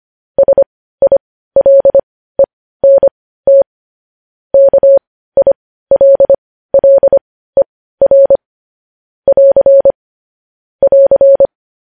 Morse code, heard as a keyed tone, about 25 wpm, spaced as for 15 wpm, 550 Hz. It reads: SILENT KILLER <AR> <AR>
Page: news headlines in morse code at 15 wpm